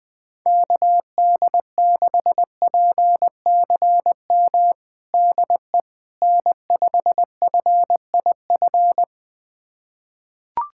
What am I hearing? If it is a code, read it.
Morse code, heard as a keyed tone, about 20 wpm, 700 Hz. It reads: KD6PCM DE N5FIF
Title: USA Callsign DE Callsign 20wpm